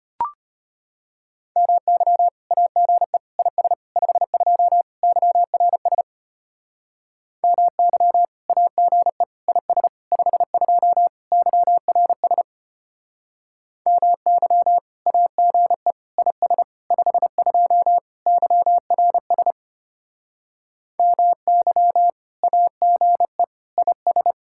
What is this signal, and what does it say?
Morse code, heard as a keyed tone, about 32 wpm, 700 Hz. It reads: MY AGE IS 52 YRS MY AGE IS 52 YRS MY AGE IS 52 YRS MY AGE IS